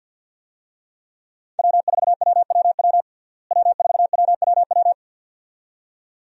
Morse code, heard as a keyed tone, about 52 wpm, 700 Hz. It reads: W4WWW W4WWW
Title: USA Call Signs - 5 Characters - Repeated 2x - 50wpm